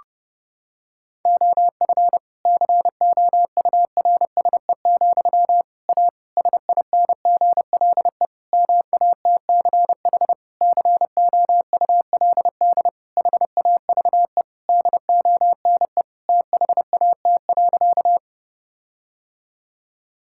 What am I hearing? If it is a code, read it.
Morse code, heard as a keyed tone, about 30 wpm, 700 Hz. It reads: OF COURSE, A SINGLE MATCH COULD HAVE DONE THAT.
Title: Sentences from Top 1600 Words 30wpm